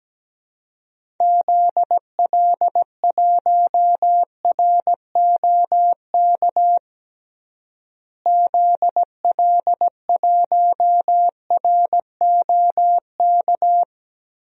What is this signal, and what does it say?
Morse code, heard as a keyed tone, about 17 wpm, 700 Hz. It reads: ZL1ROK ZL1ROK
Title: DXCC-100 Call Signs - 6 Characters - Repeated 3x - 17wpm